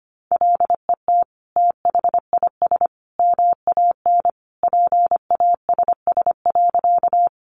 Morse code, heard as a keyed tone, about 25 wpm, 700 Hz. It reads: LET THIS MAN PASS.
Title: Sentences from Top 400 Words 25wpm